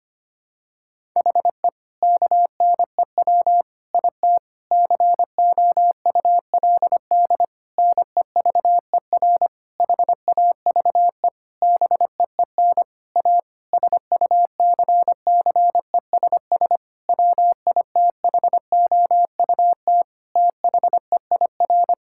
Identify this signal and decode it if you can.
Morse code, heard as a keyed tone, about 25 wpm, 700 Hz. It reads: HE KNEW IT COULD NEVER HAVE BEEN A SUCCESS WITHOUT THEIR
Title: Sentences from Top 1000 Words - Review Part 6 - 25wpm